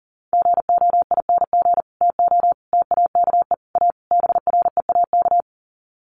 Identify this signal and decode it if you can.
Morse code, heard as a keyed tone, about 40 wpm, 700 Hz. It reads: GOING TO TAKE A BREAK
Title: Sets of 5 Words 40wpm